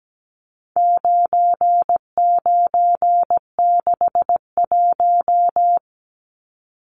Morse code, heard as a keyed tone, about 17 wpm, 700 Hz. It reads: 9961
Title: Numbers - 4-Digit 17wpm